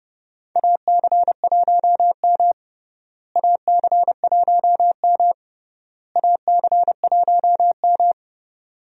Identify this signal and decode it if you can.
Morse code, heard as a keyed tone, about 32 wpm, 700 Hz. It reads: AC1M AC1M AC1M
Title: USA Call Signs - 4 Characters - Repeated 3x - 30wpm